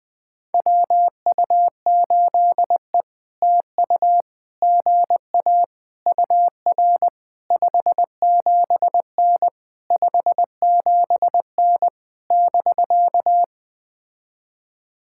Morse code, heard as a keyed tone, about 20 wpm, 700 Hz. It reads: WU8E TU GA UR 57N 57N <BK>
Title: POTA QSO Practice - 20wpm